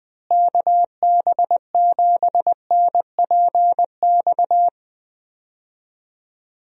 Morse code, heard as a keyed tone, about 20 wpm, 700 Hz. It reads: KB7NPX
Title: USA Call Signs - 6 Characters - Repeated 3x - 20wpm